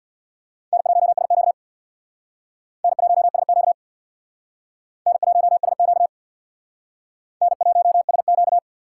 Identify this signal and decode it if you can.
Morse code, heard as a keyed tone, about 60 wpm, 700 Hz. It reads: N1SQ N1SQ N1SQ N1SQ
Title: Speed Racing - USA 1x2 Call Signs 50wpm